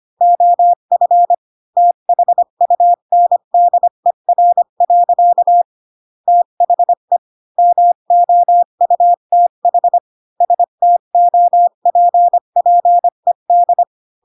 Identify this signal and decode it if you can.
Morse code, heard as a keyed tone, about 25 wpm, 700 Hz. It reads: OF THUNDER. THE MOUTH STOPPED
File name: Pinocchio / Pinocchio ch3 20 25 ik1hge